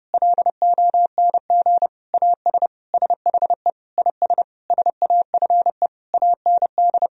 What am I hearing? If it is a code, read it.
Morse code, heard as a keyed tone, about 30 wpm, 700 Hz. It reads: LONG AS SHE IS SAFE AND
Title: Sentences from Top 1000 Words - Review Part 1 - 30wpm